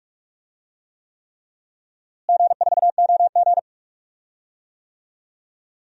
Morse code, heard as a keyed tone, about 46 wpm, 700 Hz. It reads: G4OG
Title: DXCC-100 Call Signs - 4 Characters - Repeated 3x - 45wpm